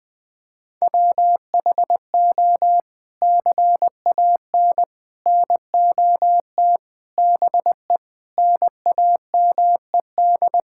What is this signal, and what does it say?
Morse code, heard as a keyed tone, about 21 wpm, 700 Hz. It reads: WHO CAN NOT BE NAMED
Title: Sets of 8 Words - 20wpm